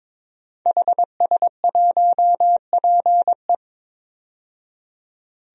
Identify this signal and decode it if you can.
Morse code, heard as a keyed tone, about 22 wpm, 700 Hz. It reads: HS1PE
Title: DXCC-100 Call Signs - 5 Characters - 22wpm